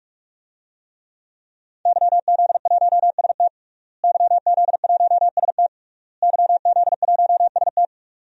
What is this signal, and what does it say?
Morse code, heard as a keyed tone, about 46 wpm, 700 Hz. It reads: YZ1ST YZ1ST YZ1ST
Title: DXCC-100 Call Signs - 5 Characters - Repeated 3x - 45wpm